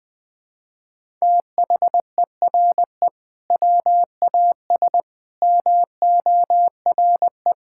Morse code, heard as a keyed tone, about 21 wpm, 700 Hz. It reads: THERE WAS MORE
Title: Sentences from Top 1000 Words - Review Part 6 - 20wpm